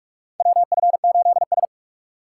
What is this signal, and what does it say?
Morse code, heard as a keyed tone, about 41 wpm, 700 Hz. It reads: WF8S
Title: USA Call Signs - 6x Character Spacing - 45wpm